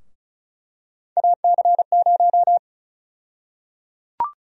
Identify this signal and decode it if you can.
Morse code, heard as a keyed tone, about 36 wpm, 700 Hz. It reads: AC0
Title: USA Call Sign Prefixes - 35wpm